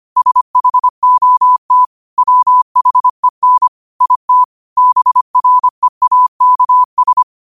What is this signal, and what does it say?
Morse code, heard as a keyed tone, about 25 wpm, 1000 Hz. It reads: SHOT WHEN IT BREAKS
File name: qotdfiles / 25wpm